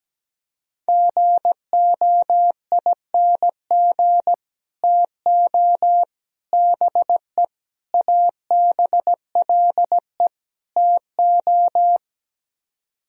Morse code, heard as a keyed tone, about 17 wpm, 700 Hz. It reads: GOING TO BE ABLE TO
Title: Speed-Racing - Sets of 5 Words 17wpm